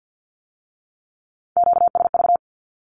Morse code, heard as a keyed tone, about 50 wpm, 700 Hz. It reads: QSV
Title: Q-Codes with Spoken Meaning - 50wpm